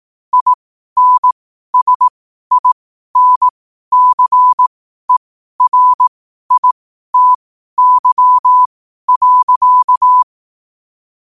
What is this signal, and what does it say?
Morse code, heard as a keyed tone, about 18 wpm, spaced as for 12 wpm, 1000 Hz. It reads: INSINCERITY.
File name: qotdfiles / 12wpm